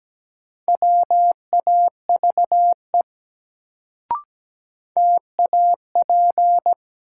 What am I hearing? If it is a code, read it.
Morse code, heard as a keyed tone, about 17 wpm, 700 Hz. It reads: WAVE TAP
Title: Sentences from Top 1400 Words - New Word Drill - 17wpm